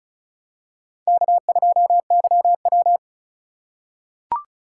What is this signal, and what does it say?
Morse code, heard as a keyed tone, about 35 wpm, 700 Hz. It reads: K2YW
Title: USA Call Signs - 4 Characters - 2x Character Spacing - 35wpm